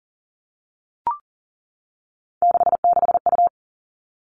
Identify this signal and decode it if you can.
Morse code, heard as a keyed tone, about 41 wpm, 700 Hz. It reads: 66U
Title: Three Challenging Letter-Number Combinations 40wpm